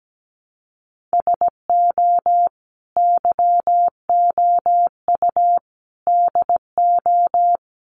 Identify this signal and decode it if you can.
Morse code, heard as a keyed tone, about 17 wpm, 700 Hz. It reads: SO YOU DO
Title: Sets of 5 Words Encore 17wpm